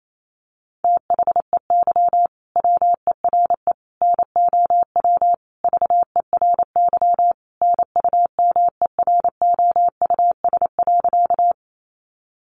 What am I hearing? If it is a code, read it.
Morse code, heard as a keyed tone, about 28 wpm, 700 Hz. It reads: THEY WERE NOW VERY NUMEROUS.
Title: Sentences from Top 1400 Words 28wpm